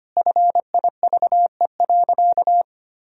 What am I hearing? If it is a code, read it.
Morse code, heard as a keyed tone, about 25 wpm, 700 Hz. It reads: FIVE.
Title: Sentences from Top 1000 Words - Review Part 2 - 25wpm